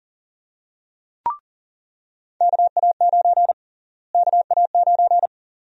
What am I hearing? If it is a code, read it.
Morse code, heard as a keyed tone, about 40 wpm, 700 Hz. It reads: KA9 KA9
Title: USA Call Sign Prefixes - Repeated 2x - 40wpm